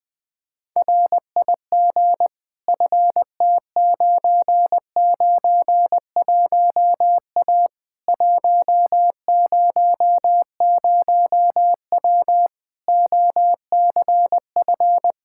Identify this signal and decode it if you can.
Morse code, heard as a keyed tone, about 20 wpm, 700 Hz. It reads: RIG FT991A 100W OCF
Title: K1USN SST QSO Practice 20wpm